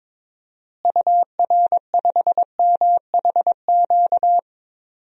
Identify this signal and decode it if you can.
Morse code, heard as a keyed tone, about 22 wpm, 700 Hz. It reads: UR5MHQ
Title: DXCC-100 Call Signs - 6 Characters - Repeated 3x - 22wpm